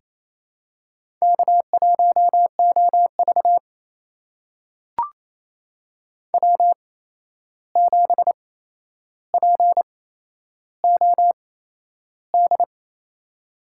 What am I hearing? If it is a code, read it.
Morse code, heard as a keyed tone, about 28 wpm, spaced as for 11 wpm, 700 Hz. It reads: K1OV W7POD
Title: USA Call Signs - 8x Character Spacing - 28wpm